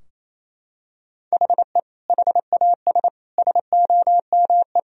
Morse code, heard as a keyed tone, about 28 wpm, 700 Hz. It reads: HE HAS SOME
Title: Sentences from Top 1000 Words - Review Part 3 - 28wpm